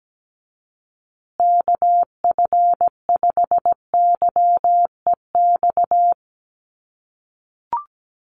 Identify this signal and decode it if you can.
Morse code, heard as a keyed tone, about 17 wpm, 700 Hz. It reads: KF5YEX
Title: USA Call Signs Repeated 3x 17wpm